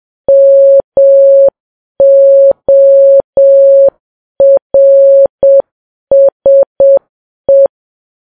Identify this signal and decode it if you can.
Morse code, heard as a keyed tone, about 7 wpm, 550 Hz. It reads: MORSE
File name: Files Web tls / Morse on9fv / morse7kps